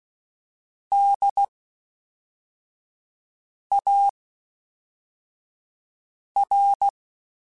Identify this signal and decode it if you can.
Morse code, heard as a keyed tone, about 16 wpm, 800 Hz. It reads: D A R
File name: Special INGs / Special INGs-mp3s / Fast ing @10